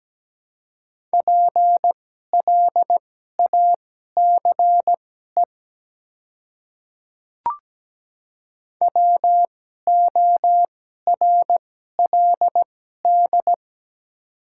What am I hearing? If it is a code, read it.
Morse code, heard as a keyed tone, about 17 wpm, spaced as for 12 wpm, 700 Hz. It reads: PLACE WORLD
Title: 2x Character Spacing - Top 200 Words - 17wpm